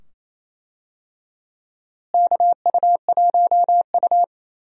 Morse code, heard as a keyed tone, about 27 wpm, 700 Hz. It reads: KU1U
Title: USA Call Signs - 4 Characters - Repeated 2x - 28wpm